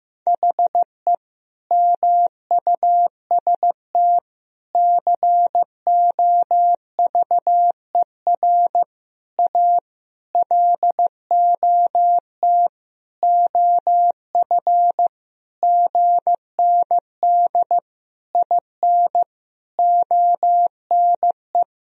Morse code, heard as a keyed tone, about 15 wpm, 700 Hz. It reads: HE MUST COVER A LOT OF GND IN ONE
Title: Sentences from Top 500 Words and Abbreviations 15wpm